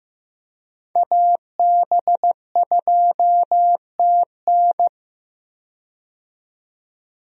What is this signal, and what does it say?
Morse code, heard as a keyed tone, about 15 wpm, 700 Hz. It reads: AB2TN
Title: USA Call Signs - 5 Characters - 15wpm